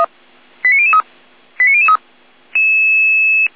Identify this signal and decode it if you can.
Five-tone selective call (ZVEI), heard as quick runs of five tones, tones of about 70 ms. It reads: 89003 89003